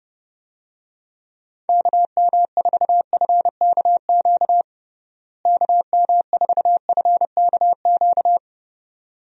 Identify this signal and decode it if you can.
Morse code, heard as a keyed tone, about 31 wpm, 700 Hz. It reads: KM4FKQ KM4FKQ
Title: USA Call Signs - 6 Characters - Repeated 2x - 30wpm